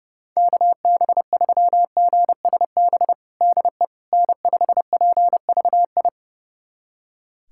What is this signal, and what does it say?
Morse code, heard as a keyed tone, about 30 wpm, 700 Hz. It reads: KB3GSB DE N5PVI
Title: USA Callsign DE Callsign 30wpm